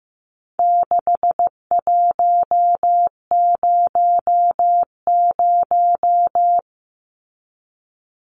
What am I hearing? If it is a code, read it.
Morse code, heard as a keyed tone, about 15 wpm, 700 Hz. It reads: 6100
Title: Numbers - 4-Digit 15wpm